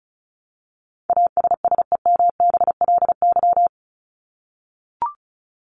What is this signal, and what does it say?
Morse code, heard as a keyed tone, about 34 wpm, 700 Hz. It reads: ASSEMBLY
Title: LY Suffix - Top 100 Words with Base Words 35wpm